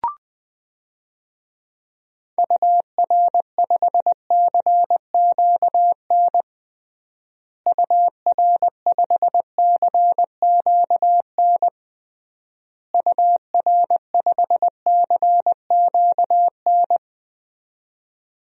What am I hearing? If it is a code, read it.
Morse code, heard as a keyed tone, about 20 wpm, 700 Hz. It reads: UR5CQN UR5CQN UR5CQN
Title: DXCC-100 Call Signs - 6 Characters - Repeated 3x - 20wpm